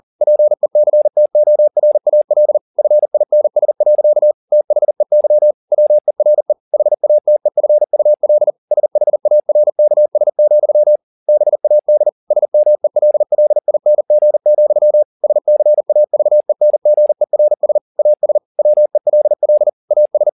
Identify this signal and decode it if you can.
Morse code, heard as a keyed tone, about 40 wpm, 600 Hz. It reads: PECTORAL FINS. THEY WERE HATEFUL SHARKS, BAD SMELLING, SCAVENGERS AS WELL AS
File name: the old man and the sea 2 40wpm